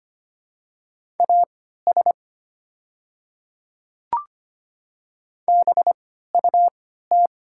Morse code, heard as a keyed tone, about 25 wpm, spaced as for 14 wpm, 700 Hz. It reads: AS BUT
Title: 3x Character Spacing - Top 100 Words - 25wpm